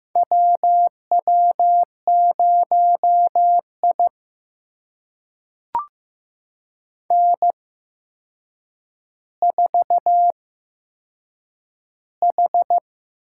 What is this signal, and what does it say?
Morse code, heard as a keyed tone, about 15 wpm, spaced as for 7 wpm, 700 Hz. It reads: WW0I N4H
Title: USA Call Signs - 4 Characters - 8x Character Spacing - 15wpm